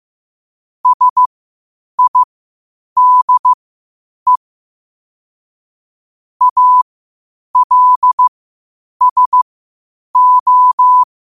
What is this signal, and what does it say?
Morse code, heard as a keyed tone, about 15 wpm, spaced as for 8 wpm, 1000 Hz. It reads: SIDE ALSO
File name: qotdfiles / 8wpm